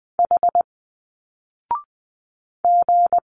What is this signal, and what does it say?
Morse code, heard as a keyed tone, about 20 wpm, 700 Hz. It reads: H G